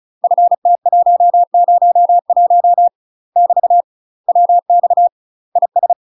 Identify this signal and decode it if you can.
Morse code, heard as a keyed tone, about 36 wpm, 700 Hz. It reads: FT101 <BT> WX IS
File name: UK MASTER QSOs TRAINING / 35WPM / UK MASTER QSOs 0038